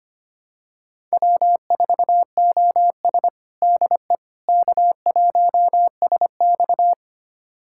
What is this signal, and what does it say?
Morse code, heard as a keyed tone, about 25 wpm, 700 Hz. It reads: W4OS DE K1SX
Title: USA Callsign DE Callsign - 4 Characters - 25wpm